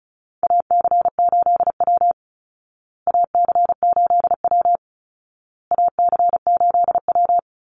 Morse code, heard as a keyed tone, about 34 wpm, 700 Hz. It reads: AC8W AC8W AC8W
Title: Call Signs Repeated 2x - 35wpm